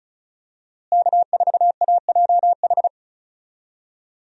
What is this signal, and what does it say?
Morse code, heard as a keyed tone, about 34 wpm, 700 Hz. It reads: K4AJH
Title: USA Call Signs - 5 Characters - Repeated 2x - 35wpm